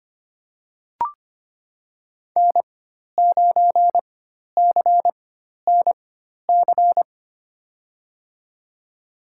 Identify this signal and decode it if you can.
Morse code, heard as a keyed tone, about 25 wpm, spaced as for 12 wpm, 700 Hz. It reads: N9CNC
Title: USA Call Signs - 4x Character Spacing - 25wpm